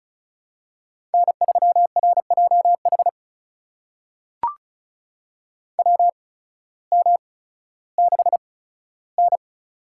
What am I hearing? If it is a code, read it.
Morse code, heard as a keyed tone, about 36 wpm, spaced as for 16 wpm, 700 Hz. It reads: N3RJH WM6N
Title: USA Call Signs - 8x Character Spacing - 35wpm